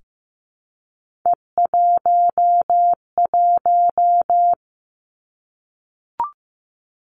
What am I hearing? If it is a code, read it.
Morse code, heard as a keyed tone, about 15 wpm, 700 Hz. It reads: E11
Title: Three Challenging Letter-Number Combinations 15wpm